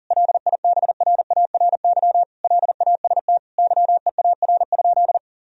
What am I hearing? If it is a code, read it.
Morse code, heard as a keyed tone, about 40 wpm, 700 Hz. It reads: LIBRARY LAST YEAR?
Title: Sentences from Top 1000 Words - Review Part 4 - 40wpm